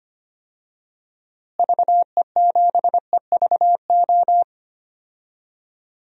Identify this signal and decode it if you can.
Morse code, heard as a keyed tone, about 25 wpm, 700 Hz. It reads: VE7EVO
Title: DXCC-100 Call Signs - 6 Characters - Repeated 2x - 25wpm